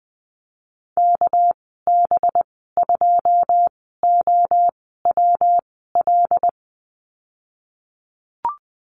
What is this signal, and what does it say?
Morse code, heard as a keyed tone, about 20 wpm, spaced as for 15 wpm, 700 Hz. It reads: KB2OWL
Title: 200 Lesson 32g - Letter k - Call Signs 20wpm